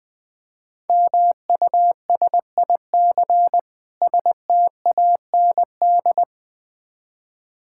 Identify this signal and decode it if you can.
Morse code, heard as a keyed tone, about 20 wpm, 700 Hz. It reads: MUSIC STAND
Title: Difficult Words - Speed-Racing with Hints - 20wpm